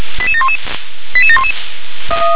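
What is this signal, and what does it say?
Five-tone selective call (ZVEI), heard as quick runs of five tones, tones of about 70 ms. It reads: 88711 88711
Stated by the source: ZVEI five-tone selective call 88704 received via SDR.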